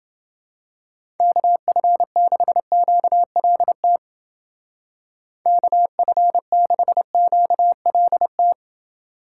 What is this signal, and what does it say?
Morse code, heard as a keyed tone, about 29 wpm, 700 Hz. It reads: KF6QLT KF6QLT